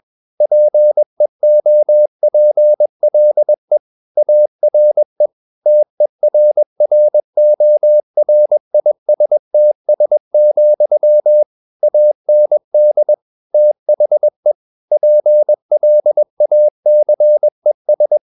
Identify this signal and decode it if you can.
Morse code, heard as a keyed tone, about 21 wpm, 600 Hz. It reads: PEOPLE ARE TERRORISTS, AND THE PLACES
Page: CW med Gnister